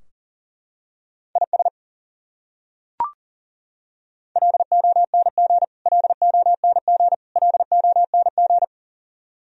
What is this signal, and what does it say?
Morse code, heard as a keyed tone, about 40 wpm, 700 Hz. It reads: IS LONG LONG LONG
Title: Top 100 Words Repeated 40wpm